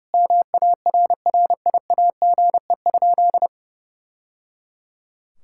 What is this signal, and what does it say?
Morse code, heard as a keyed tone, about 30 wpm, 700 Hz. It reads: MARRIAGE?
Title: Sentences from Top 1000 Words - Review Part 5 - 30wpm